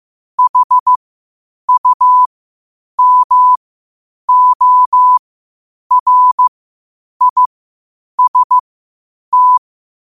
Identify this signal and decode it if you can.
Morse code, heard as a keyed tone, about 15 wpm, spaced as for 8 wpm, 1000 Hz. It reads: HUMORIST
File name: qotdfiles / 8wpm